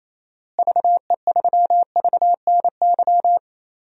Morse code, heard as a keyed tone, about 28 wpm, 700 Hz. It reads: VE3VNY
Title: DXCC-100 Call Signs - 6 Characters - Repeated 3x - 28wpm